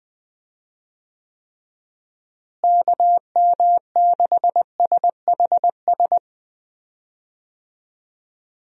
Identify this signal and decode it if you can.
Morse code, heard as a keyed tone, about 21 wpm, 700 Hz. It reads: KM6SHS
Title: USA Call Signs - 6 Characters - Repeated 2x - 20wpm